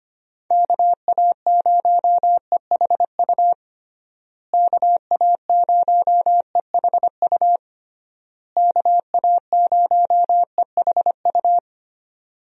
Morse code, heard as a keyed tone, about 25 wpm, 700 Hz. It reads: KA0EHU KA0EHU KA0EHU
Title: USA Call Signs - 6 Characters - Repeated 3x - 25wpm